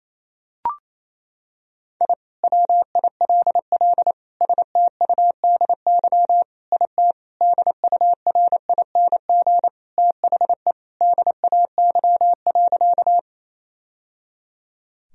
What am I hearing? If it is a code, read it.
Morse code, heard as a keyed tone, about 27 wpm, 700 Hz. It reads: I WILL STUDY IT DURING THE DAY.
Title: Sentences from Top 1000 Words - Review Part 2 - 28wpm